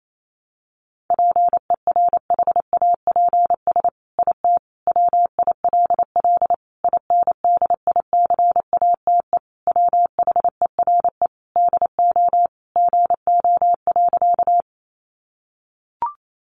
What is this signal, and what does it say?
Morse code, heard as a keyed tone, about 28 wpm, 700 Hz. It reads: PERHAPS IT WILL INDICATE WHERE DO GO.
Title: Sentences from Top 1300 Words 28wpm